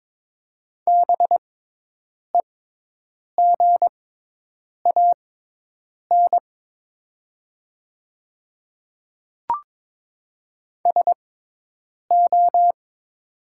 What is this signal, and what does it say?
Morse code, heard as a keyed tone, about 22 wpm, spaced as for 8 wpm, 700 Hz. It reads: BEGAN SO
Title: 6x Character Spacing - Top 300 Words - 22wpm